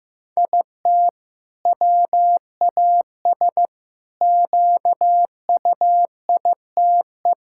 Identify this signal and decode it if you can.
Morse code, heard as a keyed tone, about 16 wpm, 700 Hz. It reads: IT WAS QUITE
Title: Sentences from Top 1000 Words - Review Part 5 - 15wpm